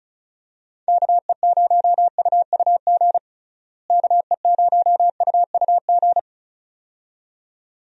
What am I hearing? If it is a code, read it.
Morse code, heard as a keyed tone, about 36 wpm, 700 Hz. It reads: KE0UUG KE0UUG
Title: USA Call Signs - 6 Characters - Repeated 2x - 35wpm